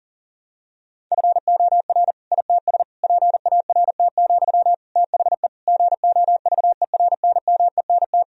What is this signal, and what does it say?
Morse code, heard as a keyed tone, about 40 wpm, 700 Hz. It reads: FOR ITS PART, THE GOVERNMENT
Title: Sentences from Top 1000 Words - Review Part 3 - 40wpm